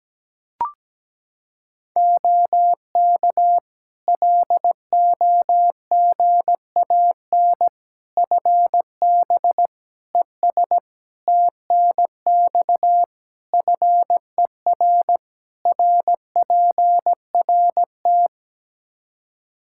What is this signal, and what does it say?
Morse code, heard as a keyed tone, about 17 wpm, 700 Hz. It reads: OK LOGAN FB ES TNX FER RPRT
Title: Practice QSO - Exchange 2 - Info - 17wpm